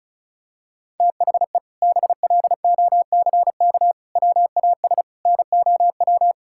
Morse code, heard as a keyed tone, about 36 wpm, 700 Hz. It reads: THE BLOCK WAS NOW
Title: Sentences from Top 1600 Words 35wpm